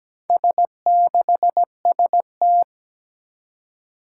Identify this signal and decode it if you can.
Morse code, heard as a keyed tone, about 17 wpm, 700 Hz. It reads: S6ST
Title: Four Challenging Letter-Number Combinations 17wpm